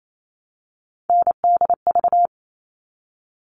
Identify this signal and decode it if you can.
Morse code, heard as a keyed tone, about 28 wpm, 700 Hz. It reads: NDV
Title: All Cut Numbers - 3-Digit 28wpm